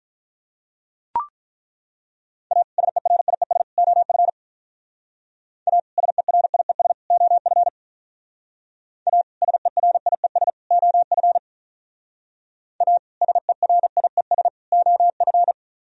Morse code, heard as a keyed tone, about 44 wpm, 700 Hz. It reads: A SERIES OF A SERIES OF A SERIES OF A SERIES OF